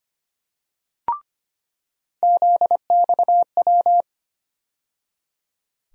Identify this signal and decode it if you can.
Morse code, heard as a keyed tone, about 25 wpm, 700 Hz. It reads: ZXW
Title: USA Call Sign Postfixes - 3 Letters 25wpm